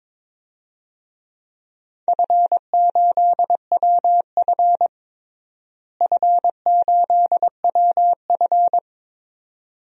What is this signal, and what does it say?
Morse code, heard as a keyed tone, about 22 wpm, 700 Hz. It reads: F8WF F8WF
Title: DXCC-100 Call Signs - 4 Characters - Repeated 2x - 22wpm